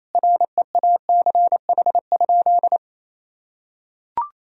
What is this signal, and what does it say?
Morse code, heard as a keyed tone, about 28 wpm, 700 Hz. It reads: REACH?
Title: Sentences from Top 1000 Words - Review Part 4 - 28wpm